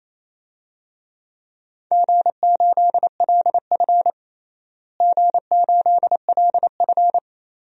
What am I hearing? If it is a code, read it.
Morse code, heard as a keyed tone, about 28 wpm, 700 Hz. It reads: G8LF G8LF
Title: DXCC-100 Call Signs - 4 Characters - Repeated 3x - 28wpm